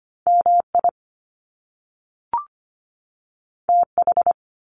Morse code, heard as a keyed tone, about 25 wpm, 700 Hz. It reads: MI TH
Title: Top Two Letter Combinations in Words 25wpm